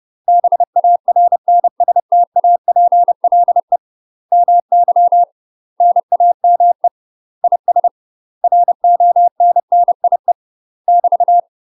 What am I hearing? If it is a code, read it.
Morse code, heard as a keyed tone, about 30 wpm, 700 Hz. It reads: BARNSTAPLE MY NAME IS RONNIE <BT>
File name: UK MASTER QSOs TRAINING/30WPM